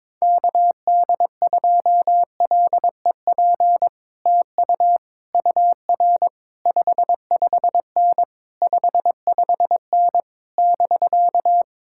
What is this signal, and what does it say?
Morse code, heard as a keyed tone, about 22 wpm, 700 Hz. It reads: KD2LEP TU UR 55N 55N <BK>
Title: Speed-Racing Common QSO Phrases - 22wpm